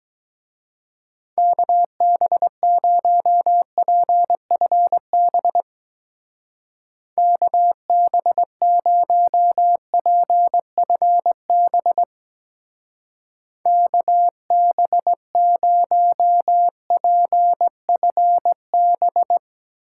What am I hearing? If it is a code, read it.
Morse code, heard as a keyed tone, about 20 wpm, 700 Hz. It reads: KB0PFB KB0PFB KB0PFB
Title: Speed Racing - USA Call Signs 17wpm